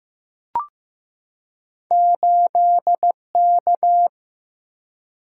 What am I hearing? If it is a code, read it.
Morse code, heard as a keyed tone, about 15 wpm, 700 Hz. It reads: 8K
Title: Single-Letter Single-Number 15wpm